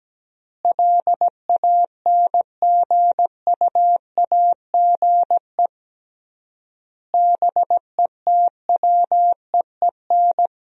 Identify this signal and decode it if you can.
Morse code, heard as a keyed tone, about 17 wpm, 700 Hz. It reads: LANGUAGE BETWEEN
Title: 3x Word-Spacing - Sentences from Top 500 Words - 17wpm